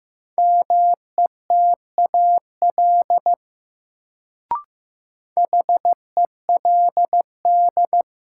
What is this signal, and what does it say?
Morse code, heard as a keyed tone, about 15 wpm, 700 Hz. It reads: METAL HELD
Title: Sentences from Top 1600 Words - New Word Drill Alt - 15wpm